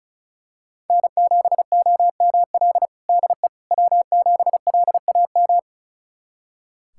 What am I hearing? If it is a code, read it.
Morse code, heard as a keyed tone, about 34 wpm, 700 Hz. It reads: N7OML DE W7LAM
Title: USA Callsign DE Callsign - 5 Characters - 35wpm